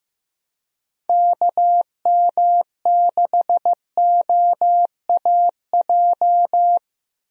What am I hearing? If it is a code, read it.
Morse code, heard as a keyed tone, about 15 wpm, 700 Hz. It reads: KM6OAJ
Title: USA Call Signs - 6 Characters - Repeated 3x - 15wpm